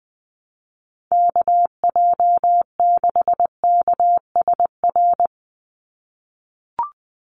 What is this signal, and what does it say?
Morse code, heard as a keyed tone, about 20 wpm, 700 Hz. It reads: KJ6KSR
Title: USA Call Signs 20wpm